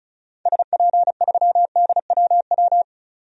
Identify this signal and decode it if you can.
Morse code, heard as a keyed tone, about 34 wpm, 700 Hz. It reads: SP3DWW
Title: DXCC-100 Call Signs - 6 Characters - 35wpm